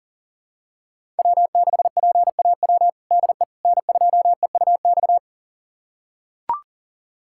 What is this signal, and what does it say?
Morse code, heard as a keyed tone, about 40 wpm, 700 Hz. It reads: W6PAW DE N2EUX
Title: USA Callsign DE Callsign - 5 Characters - 40wpm